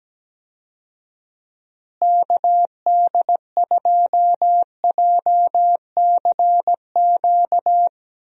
Morse code, heard as a keyed tone, about 17 wpm, 700 Hz. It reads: KD2JCQ
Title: USA Call Signs - 6 Characters - Repeated 3x - 17wpm